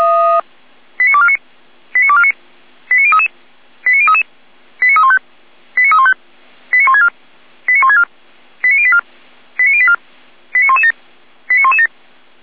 Five-tone selective call (ZVEI), heard as quick runs of five tones, tones of about 70 ms. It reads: 89249 89249 89033 89033 89315 89315 89164 89164 89084 89084 89117 89117